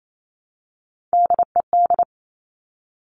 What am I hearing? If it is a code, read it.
Morse code, heard as a keyed tone, about 28 wpm, 700 Hz. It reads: DED
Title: Top Three Letter Combinations in Words 28wpm